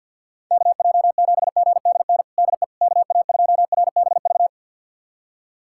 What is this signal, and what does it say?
Morse code, heard as a keyed tone, about 52 wpm, 700 Hz. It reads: KJ7GDN DE KA2RBV
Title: USA Callsign DE Callsign - 6 Characters - 50wpm